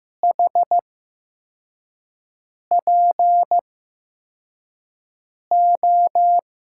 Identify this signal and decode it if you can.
Morse code, heard as a keyed tone, about 15 wpm, spaced as for 4 wpm, 700 Hz. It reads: HPO
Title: USA Call Signs - 6 Characters - 8x Character Spacing - 15wpm